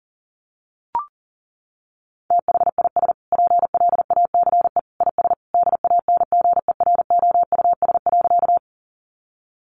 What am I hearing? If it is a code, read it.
Morse code, heard as a keyed tone, about 40 wpm, 700 Hz. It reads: THIS PLACE IS DANGEROUS.
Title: Sentences from Top 1200 Words 40wpm